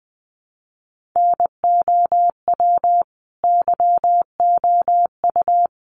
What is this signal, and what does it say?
Morse code, heard as a keyed tone, about 20 wpm, 700 Hz. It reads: NOW YOU
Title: Sentences from Top 100 Words 20wpm